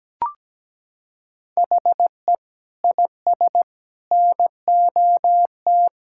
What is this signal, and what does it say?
Morse code, heard as a keyed tone, about 17 wpm, 700 Hz. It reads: HE IS NOT
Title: Sentences from Top 1600 Words 17wpm